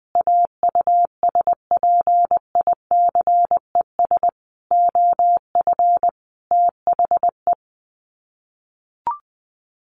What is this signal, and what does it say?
Morse code, heard as a keyed tone, about 20 wpm, 700 Hz. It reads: AUSPICES OF THE
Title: Sets of 5 Words Encore2 20wpm